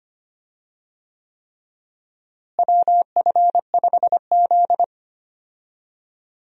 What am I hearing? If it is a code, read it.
Morse code, heard as a keyed tone, about 25 wpm, 700 Hz. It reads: WF5Z
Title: USA Call Signs - 4 Characters - Repeated 3x - 25wpm